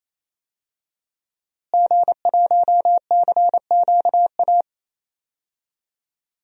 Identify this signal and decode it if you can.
Morse code, heard as a keyed tone, about 28 wpm, 700 Hz. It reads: G1CQA
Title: DXCC-100 Call Signs - 5 Characters - Repeated 3x - 28wpm